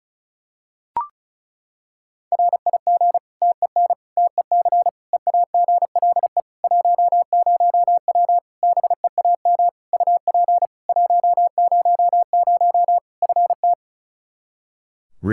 Morse code, heard as a keyed tone, about 36 wpm, 700 Hz. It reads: RIG TEN TEC EAGLE 10W BEAM UP 100 FT
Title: Practice QSO - Exchange 2 - Info - 35wpm